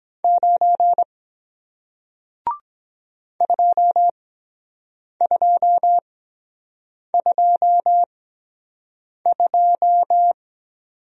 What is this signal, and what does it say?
Morse code, heard as a keyed tone, about 23 wpm, spaced as for 7 wpm, 700 Hz. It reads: 9 2222